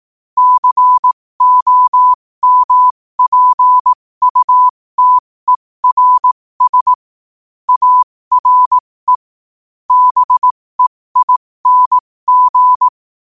Morse code, heard as a keyed tone, about 18 wpm, spaced as for 15 wpm, 1000 Hz. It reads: COMPUTERS ARE BEING
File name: qotdfiles / 15wpm